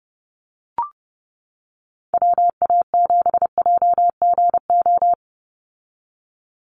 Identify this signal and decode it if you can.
Morse code, heard as a keyed tone, about 30 wpm, 700 Hz. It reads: WA7JGO
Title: USA Call Signs 30wpm